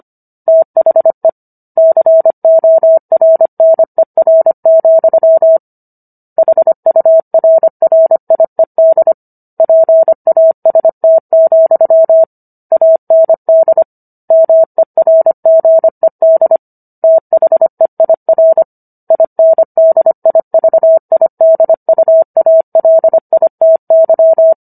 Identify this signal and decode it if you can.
Morse code, heard as a keyed tone, about 25 wpm, 650 Hz. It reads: THE CORNER, HURRIED PAST, AND MERGED THEIR INDIVIDUALITY